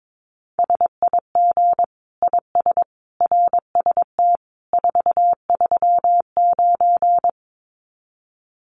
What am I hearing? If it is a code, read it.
Morse code, heard as a keyed tone, about 22 wpm, 700 Hz. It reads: SIG IS RST 439
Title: Common QSO Phrases 22wpm